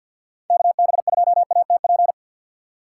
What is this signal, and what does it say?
Morse code, heard as a keyed tone, about 52 wpm, 700 Hz. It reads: KB2ATP
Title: USA Call Signs - 2x Character Spacing - 50wpm